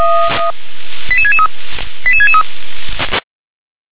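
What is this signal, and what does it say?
Five-tone selective call (ZVEI), heard as quick runs of five tones, tones of about 70 ms. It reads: 88663 88663
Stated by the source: SDR-received ZVEI five-tone call, code 88559.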